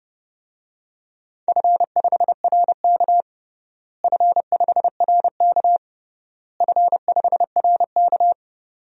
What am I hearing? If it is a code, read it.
Morse code, heard as a keyed tone, about 30 wpm, 700 Hz. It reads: F5RK F5RK F5RK
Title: DXCC-100 Call Signs - 4 Characters - Repeated 3x - 30wpm